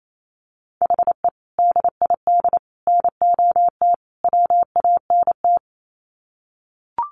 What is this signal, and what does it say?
Morse code, heard as a keyed tone, about 28 wpm, 700 Hz. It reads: HE DID NOT WANT
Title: Sets of 4 Words 28wpm